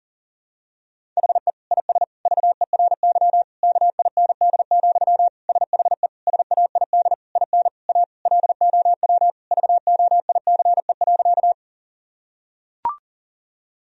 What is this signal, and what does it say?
Morse code, heard as a keyed tone, about 40 wpm, 700 Hz. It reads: HE IS VERY KIND, SHE SAID IN A LOW VOICE.
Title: Sentences from Top 1000 Words - Review Part 1 - 40wpm